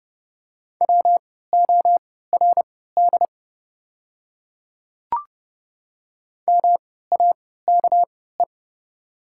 Morse code, heard as a keyed tone, about 31 wpm, spaced as for 17 wpm, 690 Hz. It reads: WORD MAKE
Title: 3x Character Spacing - Top 200 Words - 30wpm